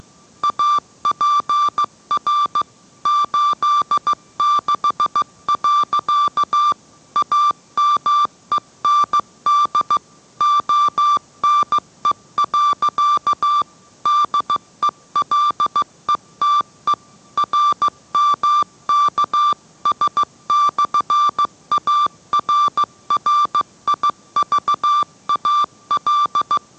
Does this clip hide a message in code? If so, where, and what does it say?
Yes: Morse code from 0.4 to 26.6 s, heard as a keyed tone, about 15 wpm, 1180 Hz. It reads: APR 86. AMEND ONE. DELETE RMKS/ARRIVAL